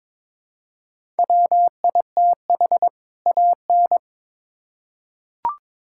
Morse code, heard as a keyed tone, about 22 wpm, 700 Hz. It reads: WITH AN
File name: Build A Phrase - 2-Word Phrases - 22wpm